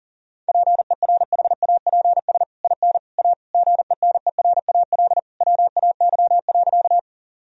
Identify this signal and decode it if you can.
Morse code, heard as a keyed tone, about 37 wpm, 700 Hz. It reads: PERHAPS IN A GENERAL WAY.
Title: Sentences from Top 1000 Words - Review Part 3 - 40wpm